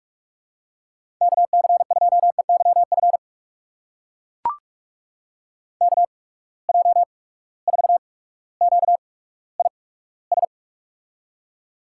Morse code, heard as a keyed tone, about 42 wpm, spaced as for 19 wpm, 700 Hz. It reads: KC2EYF KJ4QIS
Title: USA Call Signs - 6 Characters - 8x Character Spacing - 45wpm